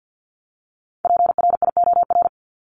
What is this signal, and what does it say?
Morse code, heard as a keyed tone, about 50 wpm, 700 Hz. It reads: PRIOR
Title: Top 1501-1750 Words 50wpm